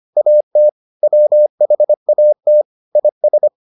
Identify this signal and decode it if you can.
Morse code, heard as a keyed tone, about 25 wpm, 600 Hz. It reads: AT WHAT IS